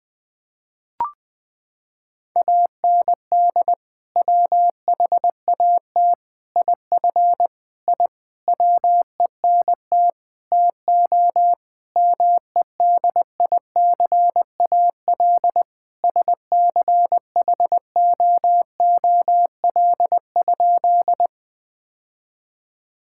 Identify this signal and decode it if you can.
Morse code, heard as a keyed tone, about 19 wpm, 700 Hz. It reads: AND WHAT IF I WENT TO MEDICAL SCHOOL?
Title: Sentences from Top 1000 Words - Review Part 5 - 20wpm